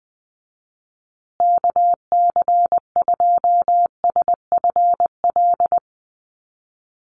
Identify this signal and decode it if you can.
Morse code, heard as a keyed tone, about 20 wpm, 700 Hz. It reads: KC2SFL
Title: USA Call Signs Repeated 2x 20wpm